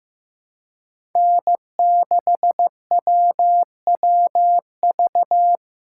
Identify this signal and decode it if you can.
Morse code, heard as a keyed tone, about 15 wpm, 700 Hz. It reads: N6WWV
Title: USA Call Signs - 5 Characters - 4x Character Spacing - 15wpm